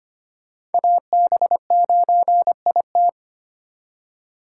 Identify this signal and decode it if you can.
Morse code, heard as a keyed tone, about 33 wpm, 700 Hz. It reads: AB9IT